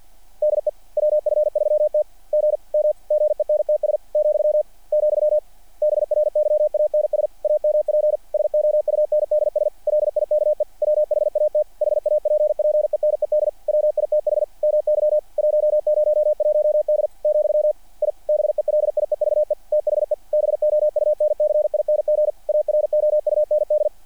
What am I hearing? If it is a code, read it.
Morse code, heard as a keyed tone, about 48 wpm, 600 Hz. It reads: DE WF3T GM GENTS , , BRYANS AMP SOUNDS LIKE WHAT HAPPENED WITH MY 101D , I BELIEVE THE BOUNCING AROUND
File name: archived / CRSnet